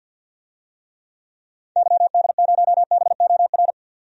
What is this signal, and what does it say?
Morse code, heard as a keyed tone, about 52 wpm, 700 Hz. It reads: YD0BOR
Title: DXCC-100 Call Signs - 6 Characters - 50wpm